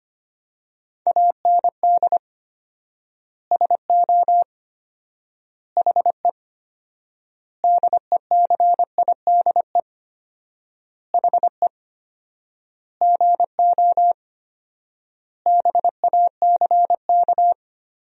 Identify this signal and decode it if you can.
Morse code, heard as a keyed tone, about 25 wpm, 700 Hz. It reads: AND SO HE DECIDE HE GO BACK
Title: 4x Word-Spacing - Sentences from Top 500 Words - 25wpm